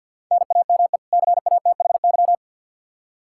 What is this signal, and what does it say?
Morse code, heard as a keyed tone, about 45 wpm, 700 Hz. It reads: NAME CATHY
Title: Practice QSO - Exchange 1 - Beginning - 50wpm